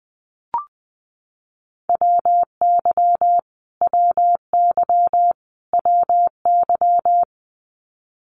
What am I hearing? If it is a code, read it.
Morse code, heard as a keyed tone, about 20 wpm, 700 Hz. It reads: WY WY WY
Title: US States Repeated 3x 20wpm